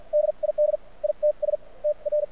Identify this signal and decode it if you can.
Morse code, heard as a keyed tone, about 50 wpm, spaced as for 40 wpm, 600 Hz. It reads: DID ITS TR
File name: archived / CRSnet